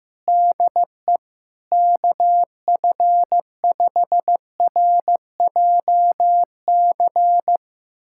Morse code, heard as a keyed tone, about 15 wpm, 700 Hz. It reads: DE KF5RJC
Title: USA Callsign DE Callsign 15wpm